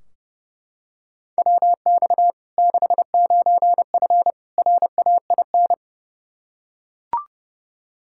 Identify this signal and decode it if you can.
Morse code, heard as a keyed tone, about 30 wpm, 700 Hz. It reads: WX 69F RAIN
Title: Practice QSO - Exchange 2 - Info - 30wpm